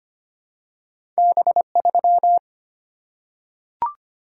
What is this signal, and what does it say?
Morse code, heard as a keyed tone, about 25 wpm, 700 Hz. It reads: B3
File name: Two Letter-Number Combinations - 25wpm